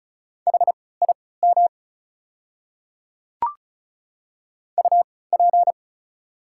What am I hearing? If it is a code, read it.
Morse code, heard as a keyed tone, about 36 wpm, spaced as for 20 wpm, 700 Hz. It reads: HIM UP
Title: 3x Character Spacing - Top 200 Words - 35wpm